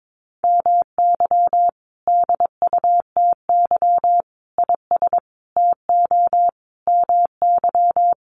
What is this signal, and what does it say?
Morse code, heard as a keyed tone, about 22 wpm, 700 Hz. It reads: MY DUTY IS TO MY
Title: Sentences from Top 1500 Words 22wpm